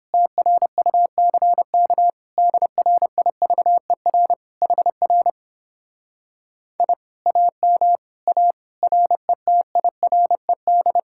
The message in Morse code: TRUCK DRIVER HR I AM A RETIRED

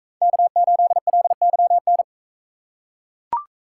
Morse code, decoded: K8LYN

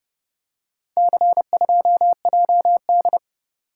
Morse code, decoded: C2JD

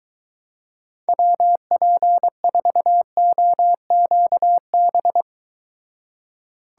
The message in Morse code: WP4OQB

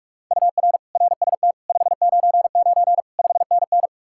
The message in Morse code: UR RST 599 5NN

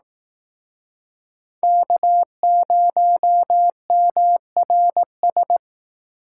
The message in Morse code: K0MRS

